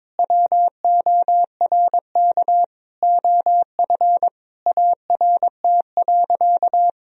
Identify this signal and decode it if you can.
Morse code, heard as a keyed tone, about 22 wpm, 700 Hz. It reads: WORK OF ART.
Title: Sentences from Top 1000 Words - Review Part 4 - 22wpm